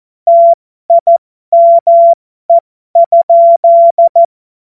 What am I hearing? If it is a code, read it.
Morse code, heard as a keyed tone, about 14 wpm, spaced as for 12 wpm, 680 Hz. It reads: TIME?
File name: Morsekurs / Witze